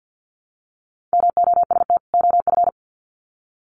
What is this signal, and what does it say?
Morse code, heard as a keyed tone, about 50 wpm, 700 Hz. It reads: MOST OF
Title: Sets of 2 Words 50wpm